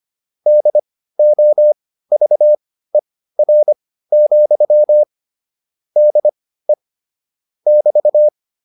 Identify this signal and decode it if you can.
Morse code, heard as a keyed tone, about 25 wpm, spaced as for 15 wpm, 600 Hz. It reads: DOVER, DE <BT>
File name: QSO308 15